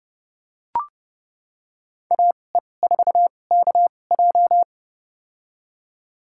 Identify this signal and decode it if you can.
Morse code, heard as a keyed tone, about 30 wpm, spaced as for 22 wpm, 700 Hz. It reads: AE4KJ
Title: USA Call Signs - 5 Characters - 2x Character Spacing - 30wpm